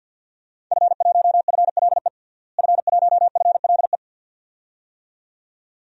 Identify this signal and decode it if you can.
Morse code, heard as a keyed tone, about 47 wpm, 700 Hz. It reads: F1FLE F1FLE
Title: DXCC-100 Call Signs - 5 Characters - Repeated 2x - 50wpm